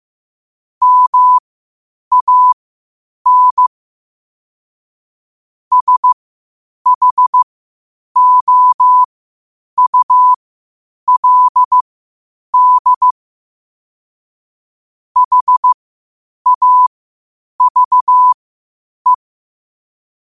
Morse code, heard as a keyed tone, about 15 wpm, spaced as for 8 wpm, 1000 Hz. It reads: MAN SHOULD HAVE
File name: qotdfiles / 8wpm